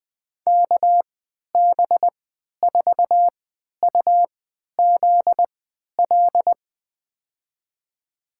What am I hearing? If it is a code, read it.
Morse code, heard as a keyed tone, about 20 wpm, spaced as for 11 wpm, 700 Hz. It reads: KB4UZL
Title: USA Call Signs - 6 Characters - 3x Character Spacing - 20wpm